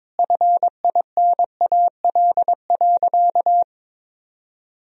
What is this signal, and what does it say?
Morse code, heard as a keyed tone, about 22 wpm, 700 Hz. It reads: FINAL.
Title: Sentences from Top 1000 Words - Review Part 2 - 22wpm